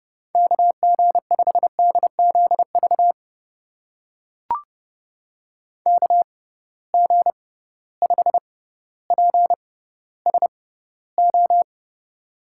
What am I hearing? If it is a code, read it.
Morse code, heard as a keyed tone, about 31 wpm, spaced as for 16 wpm, 700 Hz. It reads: KG5DZV KG5PSO
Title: USA Call Signs - 6 Characters - 6x Character Spacing - 30wpm